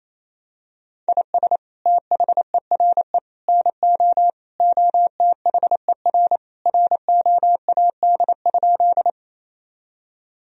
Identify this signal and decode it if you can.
Morse code, heard as a keyed tone, about 28 wpm, 700 Hz. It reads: IS THERE NO OTHER ROAD?
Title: Sentences from Top 1000 Words - Review Part 2 - 28wpm